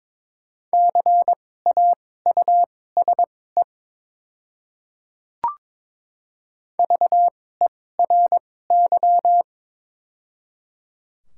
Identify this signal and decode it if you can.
Morse code, heard as a keyed tone, about 22 wpm, spaced as for 16 wpm, 700 Hz. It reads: CAUSE VERY
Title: 2x Character Spacing - Top 300 Words - 22wpm